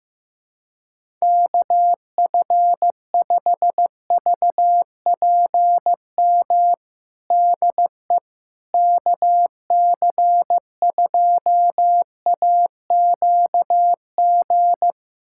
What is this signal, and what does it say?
Morse code, heard as a keyed tone, about 15 wpm, 700 Hz. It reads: KF5VPM DE KC2AQG
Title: USA Callsign DE Callsign - 6 Characters - 15wpm